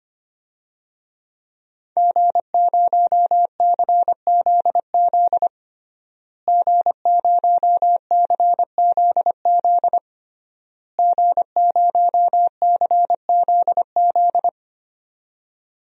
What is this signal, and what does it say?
Morse code, heard as a keyed tone, about 25 wpm, 700 Hz. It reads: G0CZZ G0CZZ G0CZZ
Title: DXCC-100 Call Signs - 5 Characters - Repeated 3x - 25wpm